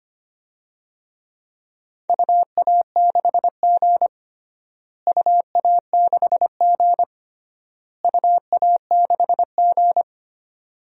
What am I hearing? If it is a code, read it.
Morse code, heard as a keyed tone, about 25 wpm, 700 Hz. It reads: UA6G UA6G UA6G
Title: DXCC-100 Call Signs - 4 Characters - Repeated 3x - 25wpm